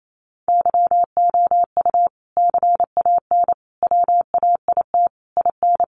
Morse code, heard as a keyed tone, about 28 wpm, 700 Hz. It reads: YOU CAN WAIT IN
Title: Sentences from Top 1400 Words 28wpm